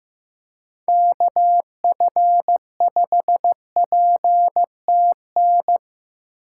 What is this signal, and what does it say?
Morse code, heard as a keyed tone, about 15 wpm, 700 Hz. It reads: KF5PTN
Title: USA Call Signs - 6 Characters - 3x Character Spacing - 15wpm